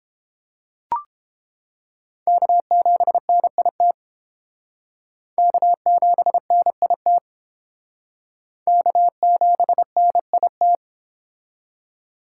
Morse code, heard as a keyed tone, about 31 wpm, 700 Hz. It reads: K7NIT K7NIT K7NIT